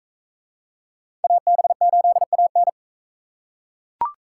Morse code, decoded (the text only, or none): AB8AN